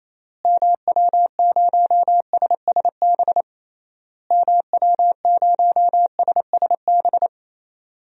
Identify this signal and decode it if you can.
Morse code, heard as a keyed tone, about 28 wpm, 700 Hz. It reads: MW0SSB MW0SSB
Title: DXCC-100 Call Signs - 6 Characters - Repeated 3x - 28wpm